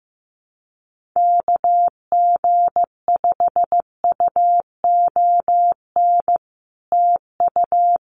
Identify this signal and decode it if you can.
Morse code, heard as a keyed tone, about 15 wpm, 700 Hz. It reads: KG5UON TU
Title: Common QSO Phrases Encore 15wpm